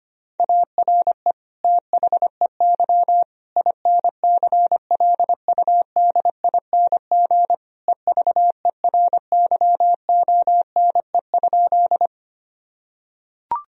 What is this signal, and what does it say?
Morse code, heard as a keyed tone, about 25 wpm, 700 Hz. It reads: ARE THEY INCLUDING EVERYONE?
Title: Sentences from Top 1000 Words - Review Part 4 - 25wpm